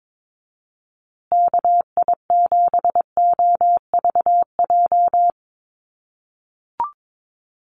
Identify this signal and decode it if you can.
Morse code, heard as a keyed tone, about 22 wpm, 700 Hz. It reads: KI7OVJ
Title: USA Call Signs Repeated 3x 22wpm